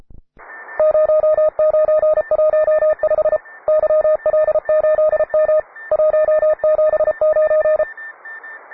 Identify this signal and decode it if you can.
Morse code, heard as a keyed tone, about 33 wpm, 620 Hz. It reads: 0915 YL8M 179